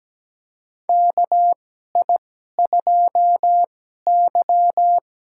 Morse code, heard as a keyed tone, about 17 wpm, spaced as for 12 wpm, 700 Hz. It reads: KI2Y
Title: USA Call Signs - 4 Characters - 2x Character Spacing - 17wpm